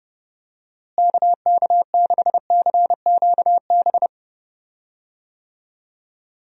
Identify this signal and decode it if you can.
Morse code, heard as a keyed tone, about 29 wpm, 700 Hz. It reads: KK6CQB
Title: USA Call Signs - 6 Characters - Repeated 2x - 30wpm